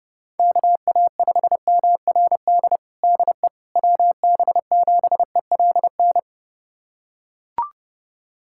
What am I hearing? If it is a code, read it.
Morse code, heard as a keyed tone, about 30 wpm, 700 Hz. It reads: KA5MRD DE WB7ELN
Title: USA Callsign DE Callsign 30wpm